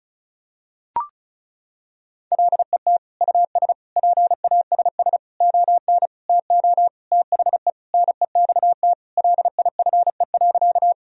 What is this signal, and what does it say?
Morse code, heard as a keyed tone, about 34 wpm, 700 Hz. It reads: LET US PASS ON TO THE NEXT LIFE.
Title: Sentences from Top 1000 Words - Review Part 1 - 35wpm